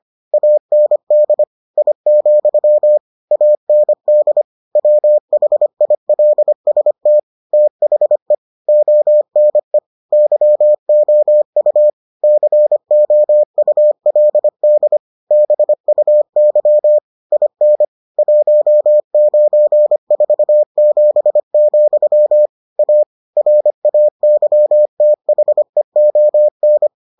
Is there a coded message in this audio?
Yes: Morse code from 0.3 to 26.9 s, heard as a keyed tone, about 25 wpm, 600 Hz. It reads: AND I, AND WHILST THE ONE YOU COULD BUY IN 1947, A RAYTHEON